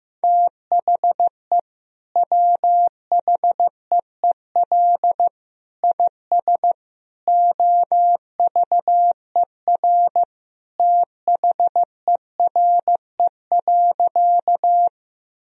Morse code, heard as a keyed tone, about 15 wpm, 700 Hz. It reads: THE WHEEL IS OVER THERE.
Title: Sentences from Top 1000 Words - Review Part 3 - 15wpm